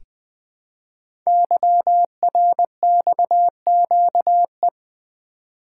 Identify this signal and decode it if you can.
Morse code, heard as a keyed tone, about 20 wpm, 700 Hz. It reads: YRXQE
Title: Code Groups - 5 Letters - 20wpm